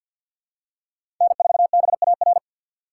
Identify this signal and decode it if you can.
Morse code, heard as a keyed tone, about 51 wpm, 700 Hz. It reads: N4BAR
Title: USA Call Signs - 6x Character Spacing - 50wpm